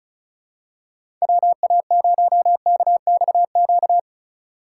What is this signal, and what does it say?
Morse code, heard as a keyed tone, about 35 wpm, 690 Hz. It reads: WA0KXQ
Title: USA Call Signs - 6 Characters - 3x Character Spacing - 35wpm